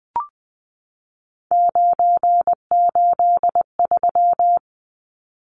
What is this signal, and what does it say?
Morse code, heard as a keyed tone, about 20 wpm, 700 Hz. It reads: 983
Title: Numbers - 3-Digit 20wpm